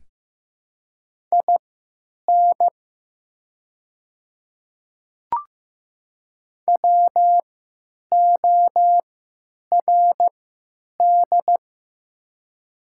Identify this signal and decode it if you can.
Morse code, heard as a keyed tone, about 15 wpm, spaced as for 9 wpm, 700 Hz. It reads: IN WORD
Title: 3x Character Spacing - Top 100 Words - 15wpm